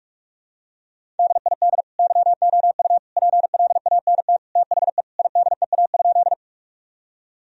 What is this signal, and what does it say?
Morse code, heard as a keyed tone, about 41 wpm, 700 Hz. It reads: DID YOU PLANT THE IDEA?
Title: Sentences from Top 1000 Words - Review Part 1 - 45wpm